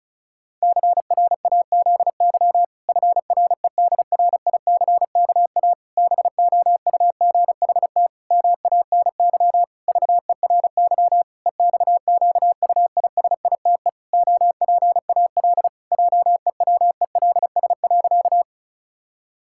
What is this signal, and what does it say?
Morse code, heard as a keyed tone, about 34 wpm, 700 Hz. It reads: CRAZY FREDRICKA BOUGHT MANY VERY EXQUISITE OPAL JEWELS.
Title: Pangram Challenge v2 - 35wpm